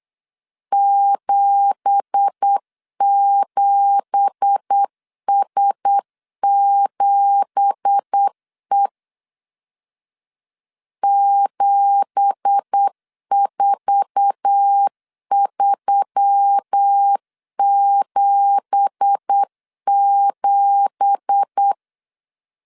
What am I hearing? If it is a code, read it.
Morse code, heard as a keyed tone, about 8 wpm, 800 Hz. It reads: 77S7E 74377
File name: LEZIONI / 9 2867 8wpm 3440